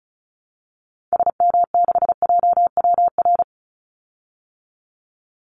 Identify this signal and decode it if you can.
Morse code, heard as a keyed tone, about 34 wpm, 700 Hz. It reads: SM6JWR
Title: Call Signs Repeated 3x - 35wpm